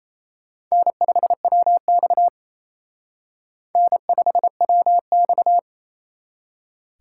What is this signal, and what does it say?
Morse code, heard as a keyed tone, about 32 wpm, 700 Hz. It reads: N5WX N5WX
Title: Speed Racing - USA Call Signs 28wpm